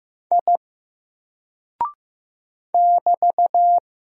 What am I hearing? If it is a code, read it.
Morse code, heard as a keyed tone, about 15 wpm, 700 Hz. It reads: I <BT>
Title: Single Character (Letter:Number:Punctuation:Prosign) 15wpm